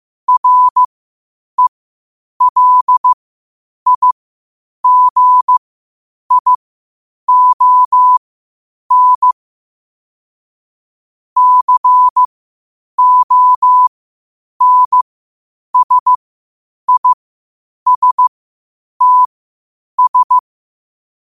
Morse code, heard as a keyed tone, about 15 wpm, spaced as for 8 wpm, 1000 Hz. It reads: RELIGION CONSISTS
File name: qotdfiles / 8wpm